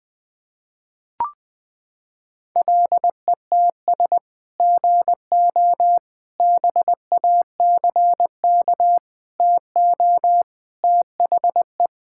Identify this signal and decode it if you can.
Morse code, heard as a keyed tone, about 21 wpm, 700 Hz. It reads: LETS GO BACK TO THE
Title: Sentences with 3-Word Phrases - 20wpm